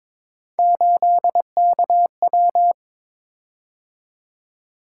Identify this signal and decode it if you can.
Morse code, heard as a keyed tone, about 22 wpm, 700 Hz. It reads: 8KW
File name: practice / 22wpm / Three Letter-Number Combinations - 22wpm